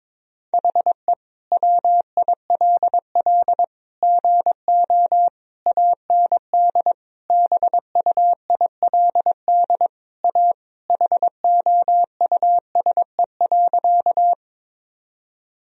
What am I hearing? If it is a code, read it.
Morse code, heard as a keyed tone, about 21 wpm, 700 Hz. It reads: HE WILL GO AND BUILD A HOUSE.
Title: Sentences from Top 1000 Words - Review Part 1 - 22wpm